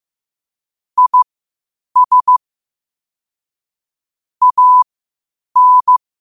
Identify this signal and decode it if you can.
Morse code, heard as a keyed tone, about 15 wpm, spaced as for 8 wpm, 1000 Hz. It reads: IS AN
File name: qotdfiles / 8wpm